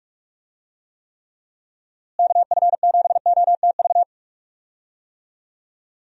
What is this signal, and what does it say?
Morse code, heard as a keyed tone, about 46 wpm, 700 Hz. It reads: KF7OTV
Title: USA Call Signs - 6 Characters - Repeated 2x - 45wpm